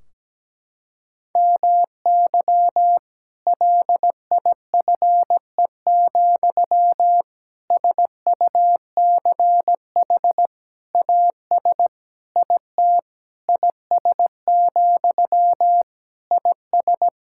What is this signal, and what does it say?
Morse code, heard as a keyed tone, about 17 wpm, 700 Hz. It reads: MY LIFE, SUCH AS IT IS, IS
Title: Sentences from Top 1600 Words 17wpm